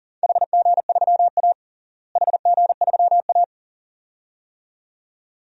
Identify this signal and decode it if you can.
Morse code, heard as a keyed tone, about 37 wpm, 700 Hz. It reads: HG3A HG3A
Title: DXCC-100 Call Signs - 4 Characters - Repeated 2x - 40wpm